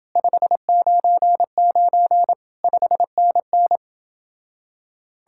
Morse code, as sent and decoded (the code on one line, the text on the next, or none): ..... ----. ----. / ..... -. -.
599 5NN